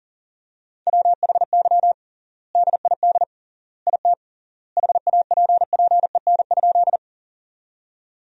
Morse code, heard as a keyed tone, about 37 wpm, 700 Hz. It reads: WHY DID IT HAPPEN?
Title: 3x Word-Spacing - Sentences from Top 400 Words - 40wpm